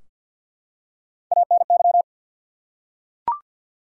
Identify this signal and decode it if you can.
Morse code, heard as a keyed tone, about 53 wpm, 700 Hz. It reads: ANY
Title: Three Letter Words - 50wpm